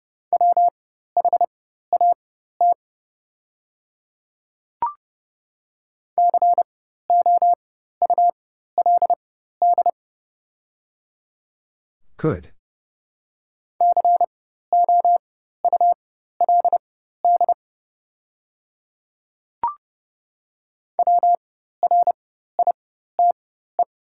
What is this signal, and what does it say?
Morse code, heard as a keyed tone, about 30 wpm, spaced as for 14 wpm, 700 Hz. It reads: WHAT COULD COULD WRITE
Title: 4x Character Spacing - Top 100 Words - 30wpm